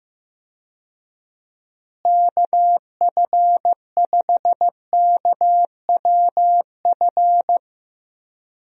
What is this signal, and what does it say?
Morse code, heard as a keyed tone, about 15 wpm, 700 Hz. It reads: KF5KWF
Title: USA Call Signs - 6 Characters - Repeated 2x - 15wpm